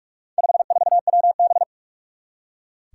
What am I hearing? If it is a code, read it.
Morse code, heard as a keyed tone, about 41 wpm, 700 Hz. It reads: 54WB
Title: Four Challenging Letter-Number Combinations 45wpm